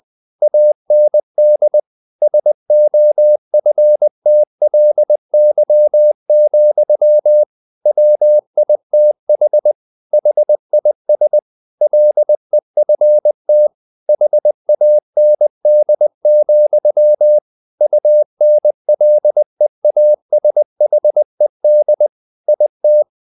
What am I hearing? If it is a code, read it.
Morse code, heard as a keyed tone, about 20 wpm, 600 Hz. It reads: AND SOFTLY, WITH HIS LEFT HAND, UNLEASHED IT
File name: the old man and the sea / the old man and the sea part 1 20wpm